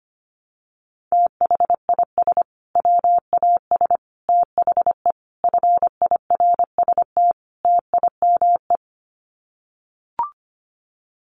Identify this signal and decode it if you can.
Morse code, heard as a keyed tone, about 25 wpm, 700 Hz. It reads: THIS WAS THE FIRST TIME
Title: Sets of 5 Words Encore 25wpm